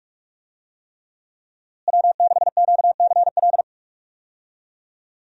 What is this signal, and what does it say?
Morse code, heard as a keyed tone, about 41 wpm, 700 Hz. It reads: W6QCL
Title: USA Call Signs - 5 Characters - Repeated 3x - 45wpm